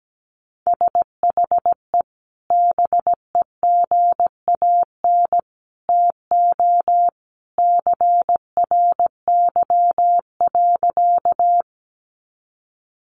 Morse code, heard as a keyed tone, about 17 wpm, 700 Hz. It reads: SHE BEGAN TO CRY.
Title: Sentences from Top 500 Words 17wpm